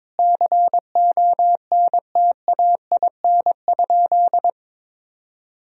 Morse code, heard as a keyed tone, about 21 wpm, 700 Hz. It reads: CONTAIN?
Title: Sentences from Top 1000 Words - Review Part 2 - 22wpm